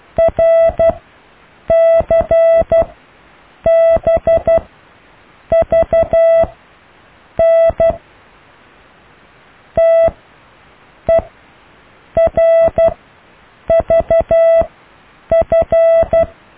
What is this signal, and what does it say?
Morse code, heard as a keyed tone, about 12 wpm, spaced as for 7 wpm, 660 Hz. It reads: RCBVN TERVF
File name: cw / IJLUM+CDRO+EKTPV+BFHNW